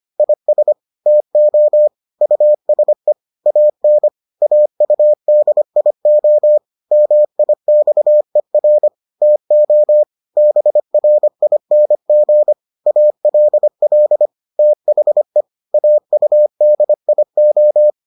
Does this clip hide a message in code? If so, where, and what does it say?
Yes: Morse code from 0.2 to 17.9 s, heard as a keyed tone, about 25 wpm, 600 Hz. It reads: IS TO USE AN AUDIO MIXER TO BRING ALL THE AUDIO